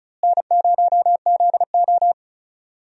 Morse code, heard as a keyed tone, about 34 wpm, 700 Hz. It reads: N0ZO
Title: USA Call Signs - 4 Characters - Repeated 3x - 35wpm